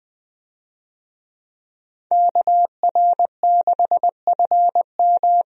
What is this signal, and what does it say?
Morse code, heard as a keyed tone, about 21 wpm, 700 Hz. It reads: KR6FM
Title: USA Call Signs - 5 Characters - 20wpm